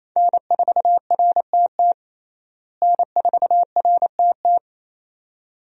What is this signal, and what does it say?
Morse code, heard as a keyed tone, about 28 wpm, 700 Hz. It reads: N4RTT N4RTT
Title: USA Call Signs - 5 Characters - Repeated 2x - 28wpm